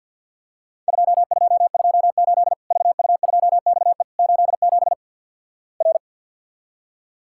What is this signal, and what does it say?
Morse code, heard as a keyed tone, about 50 wpm, 700 Hz. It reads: 2228 VU2XE 87 R
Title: Worked All Europe DX Contest - QTC Exchange - 50wpm